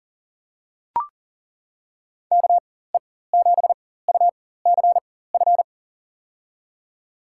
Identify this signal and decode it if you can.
Morse code, heard as a keyed tone, about 40 wpm, spaced as for 19 wpm, 700 Hz. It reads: KE7UCF
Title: USA Call Signs - 4x Character Spacing - 40wpm